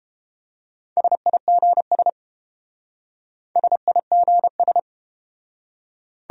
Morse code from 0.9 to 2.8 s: SIGS

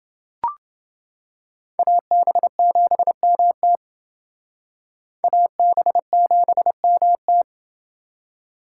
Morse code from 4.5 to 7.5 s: AB7MT

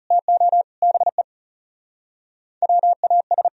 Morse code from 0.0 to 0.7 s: TO

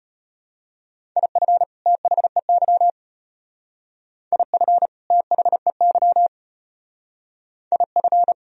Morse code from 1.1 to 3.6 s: IF THEY

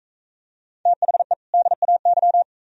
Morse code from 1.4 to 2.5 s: DAY

